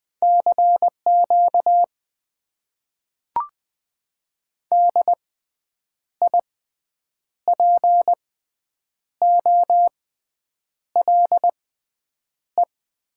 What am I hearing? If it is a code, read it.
Morse code, heard as a keyed tone, about 20 wpm, spaced as for 8 wpm, 700 Hz. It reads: CQ DIPOLE
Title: Top 90 QSO Elements - 6x Character Spacing - 20wpm